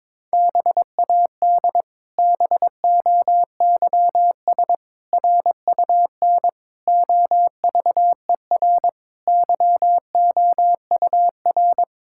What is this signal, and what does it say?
Morse code, heard as a keyed tone, about 22 wpm, 700 Hz. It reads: BAD BOYS RUN OVER YOUR
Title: Pangram Challenge v2 - 22wpm